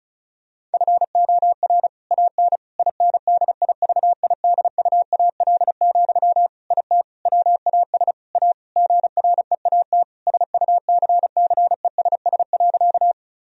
Morse code, heard as a keyed tone, about 36 wpm, 700 Hz. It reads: FOR AN INDIVIDUAL, IT WAS A GREAT SUCCESS.
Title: Sentences from Top 1000 Words - Review Part 4 - 35wpm